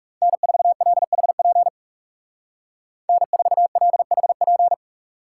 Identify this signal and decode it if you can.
Morse code, heard as a keyed tone, about 35 wpm, 700 Hz. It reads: N4LHP N4LHP
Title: Speed Racing - USA 1x3 Call Signs 30wpm